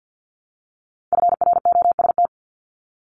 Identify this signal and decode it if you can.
Morse code, heard as a keyed tone, about 50 wpm, 700 Hz. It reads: FROST
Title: Multi-letter Phonemes Repeated 3x 50wpm